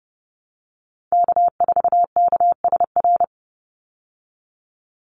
Morse code, heard as a keyed tone, about 30 wpm, 700 Hz. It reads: K4KSR
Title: Call Signs Repeated 3x - 30wpm